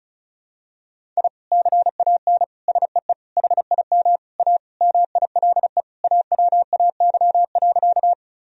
Morse code, heard as a keyed tone, about 36 wpm, 700 Hz. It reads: I CAN SEE HIM A MILE AWAY.
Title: Sentences from Top 1000 Words - Review Part 1 - 35wpm